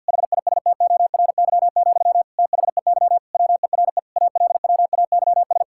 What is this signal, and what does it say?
Morse code, heard as a keyed tone, about 52 wpm, 700 Hz. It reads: HISTORY, THEY WERE ALWAYS
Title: Sentences from Top 1000 Words - Review Part 5 - 50wpm